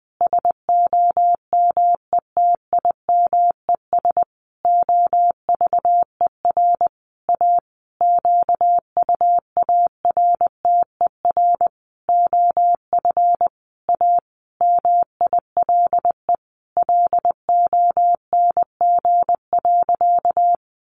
Morse code, heard as a keyed tone, about 20 wpm, 700 Hz. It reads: SOMETIMES OVER A QUARTER OF A MILE LONG.
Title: Sentences from Top 1300 Words 20wpm